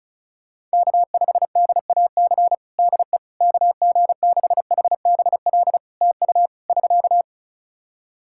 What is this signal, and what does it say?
Morse code, heard as a keyed tone, about 34 wpm, 700 Hz. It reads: K5DAC DE KG6HBL TU <SK>
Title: K1USN SST QSO Practice 35wpm